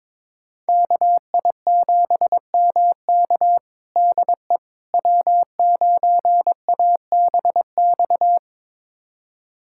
Morse code, heard as a keyed tone, about 22 wpm, 700 Hz. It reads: KI7MK DE W9ABX
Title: USA Callsign DE Callsign - 5 Characters - 22wpm